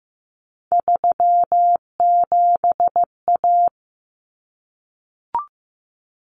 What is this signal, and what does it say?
Morse code, heard as a keyed tone, about 15 wpm, 700 Hz. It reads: 37A
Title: Common Cut Numbers - 3-Digit 15wpm